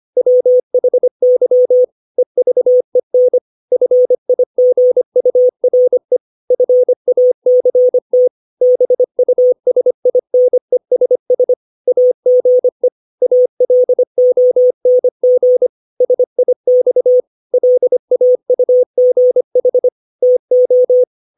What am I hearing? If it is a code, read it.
Morse code, heard as a keyed tone, about 25 wpm, 500 Hz. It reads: WHY EVEN FIGURE FACT BUSINESS AGE ALONG SIX LAUGH TO
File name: 632 words / 632 Words 25WPM